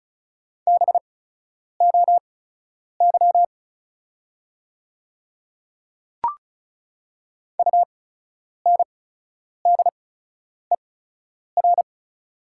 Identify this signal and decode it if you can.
Morse code, heard as a keyed tone, about 36 wpm, spaced as for 10 wpm, 700 Hz. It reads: BOY UNDER
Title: 8x Character Spacing - Top 300 Words - 35wpm